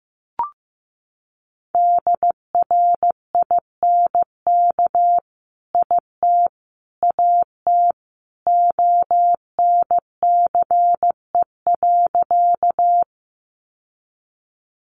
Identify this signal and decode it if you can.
Morse code, heard as a keyed tone, about 15 wpm, 700 Hz. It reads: DRINK IT AT ONCE.
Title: Sentences from Top 1500 Words 15wpm